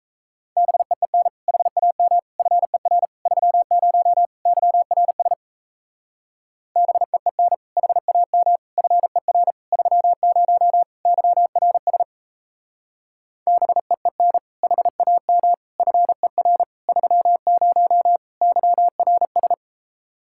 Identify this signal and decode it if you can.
Morse code, heard as a keyed tone, about 36 wpm, 700 Hz. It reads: BEEN HAM FER 30 YRS BEEN HAM FER 30 YRS BEEN HAM FER 30 YRS